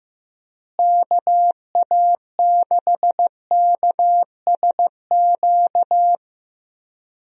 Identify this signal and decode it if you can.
Morse code, heard as a keyed tone, about 16 wpm, 700 Hz. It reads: KA6KSQ
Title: USA Call Signs - 6 Characters - Repeated 3x - 15wpm